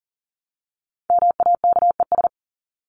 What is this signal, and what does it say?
Morse code, heard as a keyed tone, about 40 wpm, 700 Hz. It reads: MAKES
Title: ES:S Suffix - Top 100 Words with Base Words 40wpm